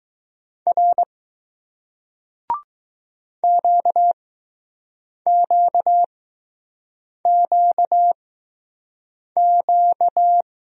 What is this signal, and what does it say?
Morse code, heard as a keyed tone, about 20 wpm, 700 Hz. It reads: R Q Q Q Q